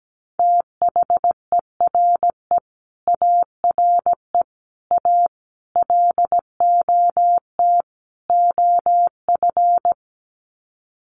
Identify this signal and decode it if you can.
Morse code, heard as a keyed tone, about 17 wpm, 700 Hz. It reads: THERE ARE A LOT OF
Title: Sets of 5 Words 17wpm